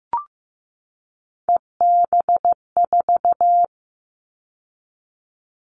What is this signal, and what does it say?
Morse code, heard as a keyed tone, about 15 wpm, 700 Hz. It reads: EB4
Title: All Cut Numbers - 3-Digit 15wpm